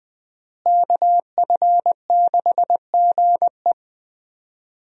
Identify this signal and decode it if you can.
Morse code, heard as a keyed tone, about 20 wpm, 700 Hz. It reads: KF6GE
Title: USA Call Signs - 5 Characters - Repeated 2x - 20wpm